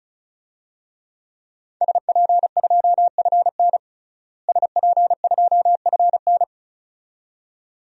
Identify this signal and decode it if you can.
Morse code, heard as a keyed tone, about 36 wpm, 700 Hz. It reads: SP2FN SP2FN
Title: DXCC-100 Call Signs - 5 Characters - Repeated 2x - 35wpm